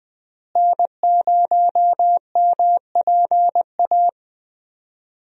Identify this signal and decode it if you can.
Morse code, heard as a keyed tone, about 20 wpm, 700 Hz. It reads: N0MPA
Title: USA Call Signs - 5 Characters - Repeated 2x - 20wpm